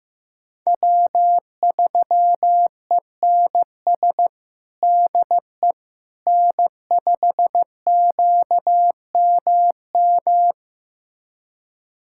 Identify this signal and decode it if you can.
Morse code, heard as a keyed tone, about 15 wpm, 700 Hz. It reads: W3ENS DE N5QMM
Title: USA Callsign DE Callsign - 5 Characters - 15wpm